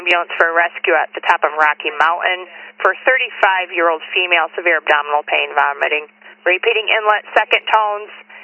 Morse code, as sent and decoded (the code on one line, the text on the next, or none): none